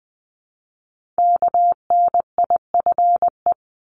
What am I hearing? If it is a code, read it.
Morse code, heard as a keyed tone, about 19 wpm, 700 Hz. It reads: KNIFE
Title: Multi-letter Phonemes Repeated 3x 20wpm